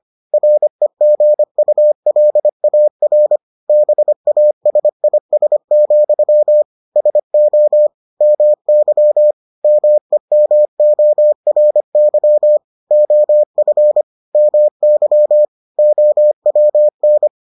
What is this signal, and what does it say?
Morse code, heard as a keyed tone, about 25 wpm, 600 Hz. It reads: REGULAR BASIS, SO MY MEMORY OF MY OWN